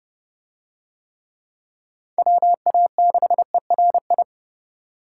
Morse code, heard as a keyed tone, about 32 wpm, 700 Hz. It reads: WA6ERI